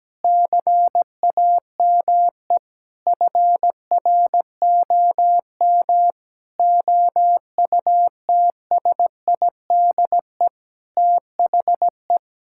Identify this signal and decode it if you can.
Morse code, heard as a keyed tone, about 17 wpm, 700 Hz. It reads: CAME FROM OUTSIDE THE
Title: Sentences from Top 1000 Words - Review Part 6 - 17wpm